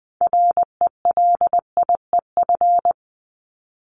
Morse code, heard as a keyed tone, about 20 wpm, 700 Hz. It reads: RELIEF
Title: Top 1501-1750 Words 20wpm